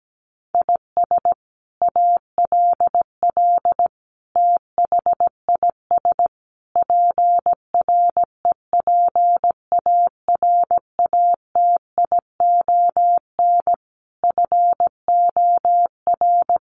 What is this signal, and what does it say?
Morse code, heard as a keyed tone, about 17 wpm, 700 Hz. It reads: IS ALL THIS PREPARATION FOR
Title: Sentences from Top 1400 Words 17wpm